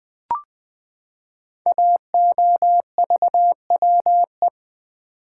Morse code, heard as a keyed tone, about 20 wpm, 700 Hz. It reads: AOVWE
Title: Code Groups - 5 Letters - 20wpm